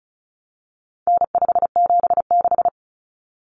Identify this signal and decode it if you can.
Morse code, heard as a keyed tone, about 34 wpm, 700 Hz. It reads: N576
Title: Common Cut Numbers - 4-Digit 35wpm